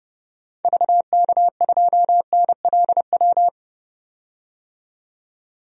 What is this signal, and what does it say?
Morse code, heard as a keyed tone, about 30 wpm, 700 Hz. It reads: VK2NLW
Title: DXCC-100 Call Signs - 6 Characters - 30wpm